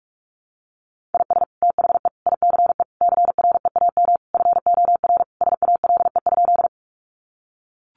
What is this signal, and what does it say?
Morse code, heard as a keyed tone, about 44 wpm, 700 Hz. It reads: IS THE ICE CREAM FOR SALE?
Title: Sentences from Top 1500 Words 45wpm